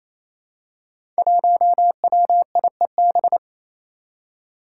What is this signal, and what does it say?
Morse code, heard as a keyed tone, about 29 wpm, 700 Hz. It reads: 1WIEB
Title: Five Challenging Letter-Number Combinations 28wpm